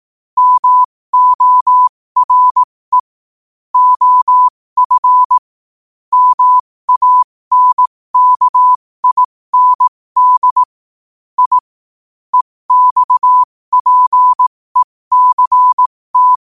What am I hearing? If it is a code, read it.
Morse code, heard as a keyed tone, about 18 wpm, spaced as for 15 wpm, 1000 Hz. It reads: MORE OF MANKIND I EXPECT
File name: qotdfiles / 15wpm